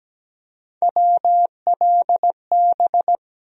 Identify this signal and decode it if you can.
Morse code, heard as a keyed tone, about 17 wpm, 700 Hz. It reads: WLB
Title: USA Call Sign Postfixes - 3 Letters 17wpm